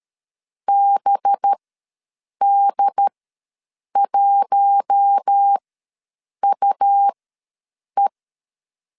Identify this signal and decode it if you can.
Morse code, heard as a keyed tone, about 13 wpm, spaced as for 7 wpm, 800 Hz. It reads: BD1UE
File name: LEZIONI / 5 LBVC / 5 LB ws 12wpm 3034